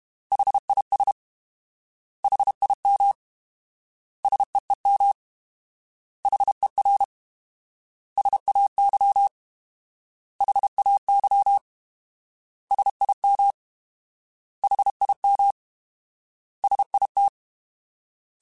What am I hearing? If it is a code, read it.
Morse code, heard as a keyed tone, about 32 wpm, 800 Hz. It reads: HIS HIM SEEM HER SAY HAY SIM HIM SIT
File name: Alphabetical / Alphabetical mp3s / S vs H @30